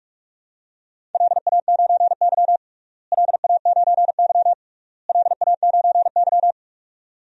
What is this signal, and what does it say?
Morse code, heard as a keyed tone, about 41 wpm, 700 Hz. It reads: LA9Y LA9Y LA9Y
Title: DXCC-100 Call Signs - 4 Characters - Repeated 3x - 45wpm